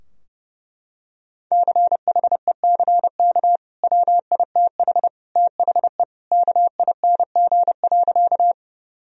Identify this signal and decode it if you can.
Morse code, heard as a keyed tone, about 31 wpm, 700 Hz. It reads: CHECK WITH THE KING.
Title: Sentences from Top 1000 Words - Review Part 3 - 30wpm